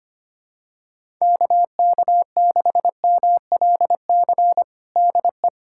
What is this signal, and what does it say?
Morse code, heard as a keyed tone, about 25 wpm, 700 Hz. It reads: KK6MLC DE
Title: USA Callsign DE Callsign 25wpm